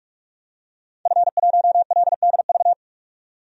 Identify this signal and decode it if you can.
Morse code, heard as a keyed tone, about 46 wpm, 700 Hz. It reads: F1LDV
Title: DXCC-100 Call Signs - 5 Characters - 45wpm